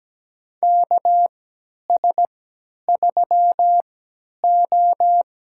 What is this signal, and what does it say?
Morse code, heard as a keyed tone, about 17 wpm, spaced as for 10 wpm, 700 Hz. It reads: KS3O
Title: USA Call Signs - 4 Characters - 3x Character Spacing - 17wpm